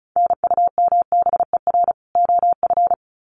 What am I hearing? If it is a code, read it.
Morse code, heard as a keyed tone, about 34 wpm, 700 Hz. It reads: NUMBER OF
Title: Sets of 2 Words 35wpm